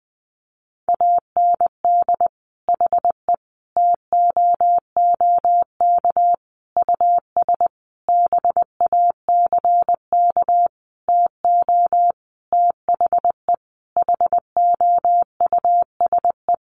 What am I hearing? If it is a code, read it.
Morse code, heard as a keyed tone, about 20 wpm, 700 Hz. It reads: AND HE TOOK US BACK TO THE HOUSE
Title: Sentences from Top 1500 Words 20wpm